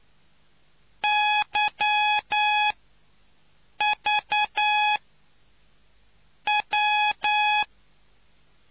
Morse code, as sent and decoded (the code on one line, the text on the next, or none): -.-- / ...- / .--
Y V W